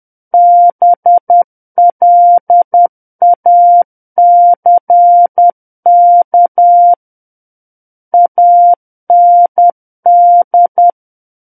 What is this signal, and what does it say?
Morse code, heard as a keyed tone, about 10 wpm, 700 Hz. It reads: BLACK AND
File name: Book One / 01-Chapter 1 - 10 WPM